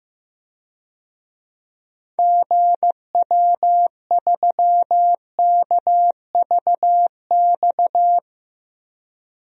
Morse code, heard as a keyed tone, about 15 wpm, 700 Hz. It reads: GW3KVX
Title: DXCC-100 Call Signs - 6 Characters - 15wpm